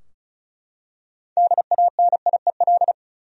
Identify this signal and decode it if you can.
Morse code, heard as a keyed tone, about 36 wpm, 700 Hz. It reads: DANIEL
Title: USA Common Names - 35wpm